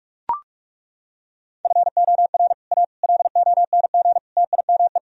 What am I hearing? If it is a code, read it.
Morse code, heard as a keyed tone, about 37 wpm, 700 Hz. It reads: FOR A LONG TIME
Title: Speed-Racing - Sets of 4 Words 30wpm